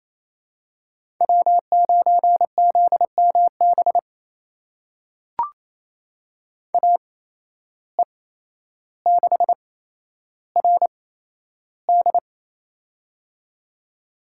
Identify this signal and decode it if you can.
Morse code, heard as a keyed tone, about 27 wpm, spaced as for 12 wpm, 700 Hz. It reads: W9ZMB AE6RD
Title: USA Call Signs - 5 Characters - 8x Character Spacing - 28wpm